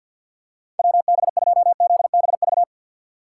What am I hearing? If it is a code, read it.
Morse code, heard as a keyed tone, about 48 wpm, 700 Hz. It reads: WB2ZBV
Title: USA Call Signs - 6 Characters - 3x Character Spacing - 50wpm